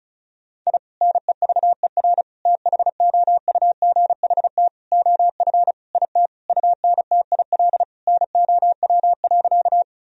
Morse code, heard as a keyed tone, about 34 wpm, 700 Hz. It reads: I NEVER THOUGHT OF IT UNTIL NOW.
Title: Sentences from Top 1000 Words - Review Part 1 - 35wpm